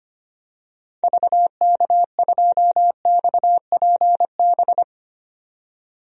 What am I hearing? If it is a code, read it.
Morse code, heard as a keyed tone, about 25 wpm, 700 Hz. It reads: VK2XPB
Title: DXCC-100 Call Signs - 6 Characters - 25wpm